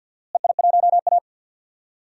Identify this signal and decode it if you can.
Morse code, heard as a keyed tone, about 51 wpm, 700 Hz. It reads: EI1A
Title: DXCC-100 Call Signs - 4 Characters - Repeated 3x - 50wpm